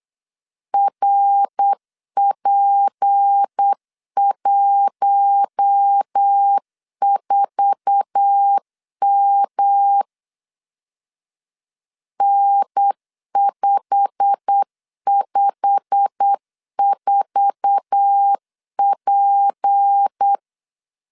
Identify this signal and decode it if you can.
Morse code, heard as a keyed tone, about 8 wpm, 800 Hz. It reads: RP14M N554P
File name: LEZIONI / 8 YJ45 / 8 YJ45 8wpm 3214